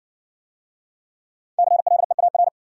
Morse code, heard as a keyed tone, about 53 wpm, 700 Hz. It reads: CLEAR